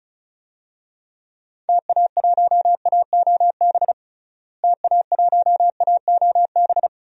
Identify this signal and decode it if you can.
Morse code, heard as a keyed tone, about 36 wpm, 700 Hz. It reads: TA1AOB TA1AOB
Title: DXCC-100 Call Signs - 6 Characters - Repeated 2x - 35wpm